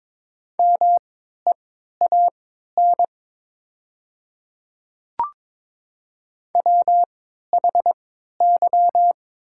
Morse code, heard as a keyed tone, about 21 wpm, spaced as for 12 wpm, 700 Hz. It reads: MEAN WHY
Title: 3x Character Spacing - Top 200 Words - 22wpm